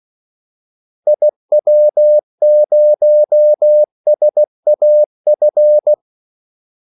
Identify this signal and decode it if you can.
Morse code, heard as a keyed tone, about 16 wpm, 600 Hz. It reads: IW0SAF
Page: Kaldesignaler 16 wpm